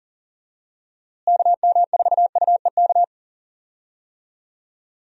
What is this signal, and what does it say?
Morse code, heard as a keyed tone, about 40 wpm, 700 Hz. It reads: KM4UEK
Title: USA Call Signs - 6 Characters - Repeated 3x - 40wpm